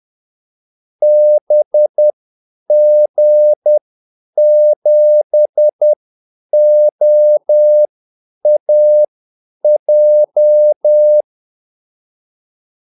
Morse code, heard as a keyed tone, about 10 wpm, spaced as for 8 wpm, 600 Hz. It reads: BG7OAJ
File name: Callsigns-10-8wpm